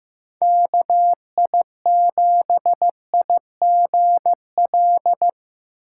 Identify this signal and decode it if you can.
Morse code, heard as a keyed tone, about 15 wpm, 700 Hz. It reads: KI7IGL
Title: USA Call Signs - 6 Characters - 4x Character Spacing - 15wpm